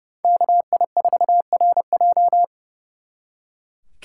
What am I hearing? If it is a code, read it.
Morse code, heard as a keyed tone, about 36 wpm, 700 Hz. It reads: KI4RJ
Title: Speed Racing - USA 2x2 Call Signs 30wpm